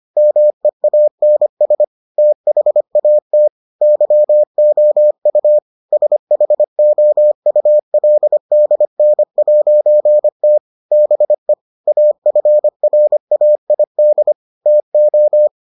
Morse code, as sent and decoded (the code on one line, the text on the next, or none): -- . .- -. ... / - .... .- - / -.-- --- ..- / ... .... --- ..- .-.. -.. -. .----. - / -... / .- ..-. .-. .- .. -.. / - ---
MEANS THAT YOU SHOULDN'T BE AFRAID TO